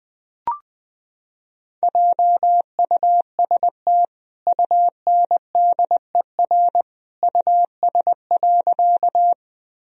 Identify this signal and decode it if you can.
Morse code, heard as a keyed tone, about 20 wpm, 700 Hz. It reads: JUST UNDER US.
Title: Sentences from Top 1000 Words - Review Part 1 - 20wpm